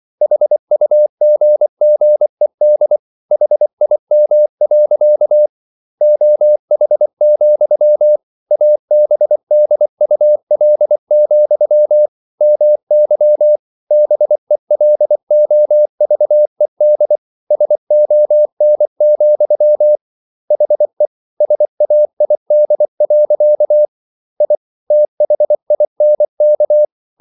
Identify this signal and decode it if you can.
Morse code, heard as a keyed tone, about 24 wpm, 600 Hz. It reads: HUGGED HIM. OH, ABDUL, MY BELOVED SON, HE SAID. I THINK